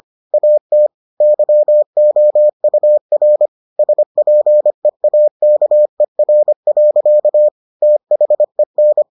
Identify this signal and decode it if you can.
Morse code, heard as a keyed tone, about 25 wpm, 600 Hz. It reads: AT YOUR SPEAKER. THEN